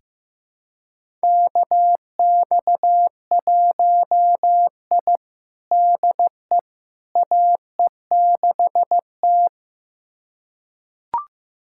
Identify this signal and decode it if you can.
Morse code, heard as a keyed tone, about 15 wpm, 700 Hz. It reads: KX1I DE AE6T
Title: USA Callsign DE Callsign - 4 Characters - 15wpm